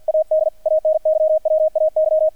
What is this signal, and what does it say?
Morse code, heard as a keyed tone, about 48 wpm, 630 Hz, with radio noise. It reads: AD ANYWAY
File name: archived / CRSnet